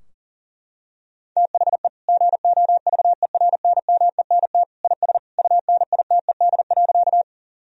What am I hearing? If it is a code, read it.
Morse code, heard as a keyed tone, about 40 wpm, 700 Hz. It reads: THE GOVERNMENT IS UNITED.
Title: Sentences from Top 1000 Words - Review Part 4 - 40wpm